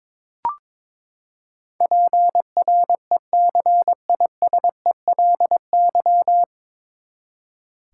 Morse code, heard as a keyed tone, about 22 wpm, 700 Hz. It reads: PRECISELY
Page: Morse Code Ninja practice 22wpm